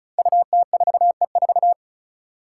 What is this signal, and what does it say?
Morse code, heard as a keyed tone, about 34 wpm, 700 Hz. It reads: UT4E4
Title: Five Challenging Letter-Number Combinations 35wpm